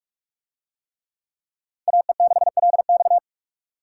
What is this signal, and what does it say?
Morse code, heard as a keyed tone, about 39 wpm, 700 Hz. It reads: AE6LX